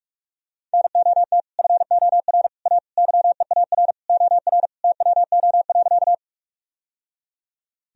Morse code, heard as a keyed tone, about 46 wpm, 700 Hz. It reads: NOT FOR A YEAR OR TWO.
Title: Sentences from Top 1000 Words - Review Part 1 - 45wpm